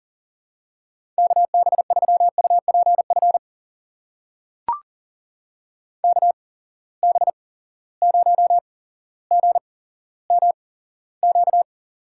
Morse code, heard as a keyed tone, about 40 wpm, spaced as for 17 wpm, 700 Hz. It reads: KB3UPF KB0GMQ
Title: USA Call Signs - 6 Characters - 8x Character Spacing - 40wpm